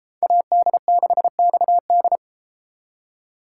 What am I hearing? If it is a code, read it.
Morse code, heard as a keyed tone, about 32 wpm, 700 Hz. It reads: AD6XD